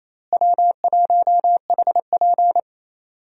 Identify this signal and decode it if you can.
Morse code, heard as a keyed tone, about 28 wpm, 700 Hz. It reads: W1HP
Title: USA Call Signs - 4 Characters - Repeated 3x - 28wpm